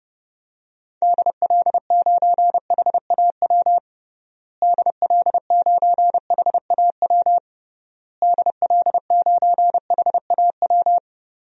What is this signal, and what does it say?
Morse code, heard as a keyed tone, about 30 wpm, 700 Hz. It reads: DL9HAW DL9HAW DL9HAW
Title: DXCC-100 Call Signs - 6 Characters - Repeated 3x - 30wpm